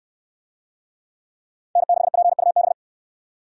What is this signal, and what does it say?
Morse code, heard as a keyed tone, about 69 wpm, 700 Hz. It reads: N6PUB